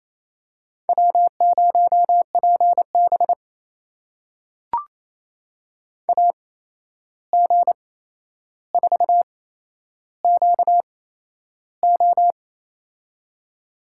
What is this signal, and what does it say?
Morse code, heard as a keyed tone, about 28 wpm, spaced as for 11 wpm, 700 Hz. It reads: W0PB AG4QO
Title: USA Call Signs - 8x Character Spacing - 28wpm